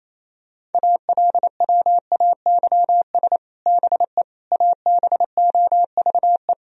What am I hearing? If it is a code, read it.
Morse code, heard as a keyed tone, about 28 wpm, 700 Hz. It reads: ALWAYS BE ABOVE
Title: Sentences from Top 1000 Words - Review Part 2 - 28wpm